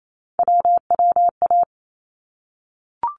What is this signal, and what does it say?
Morse code, heard as a keyed tone, about 28 wpm, 700 Hz. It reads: WWA
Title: ARRL Sections 28wpm